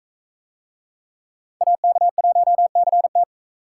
Morse code, heard as a keyed tone, about 42 wpm, 700 Hz. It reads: AK1CT